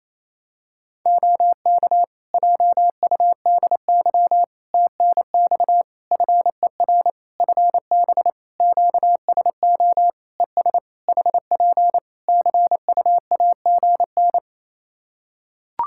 Morse code, heard as a keyed tone, about 28 wpm, 700 Hz. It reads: OK JUDY TNX FER FB QSO ES HP CUAGN
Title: Practice QSO - Exchange 4 - Ending - 28wpm